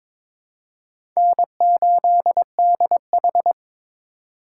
Morse code, heard as a keyed tone, about 22 wpm, 700 Hz. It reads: N8DH
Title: USA Call Signs - 4 Characters - 2x Character Spacing - 22wpm